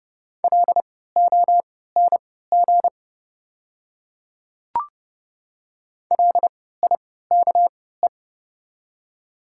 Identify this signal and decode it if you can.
Morse code, heard as a keyed tone, about 31 wpm, spaced as for 17 wpm, 690 Hz. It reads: LONG LIKE
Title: 3x Character Spacing - Top 100 Words - 30wpm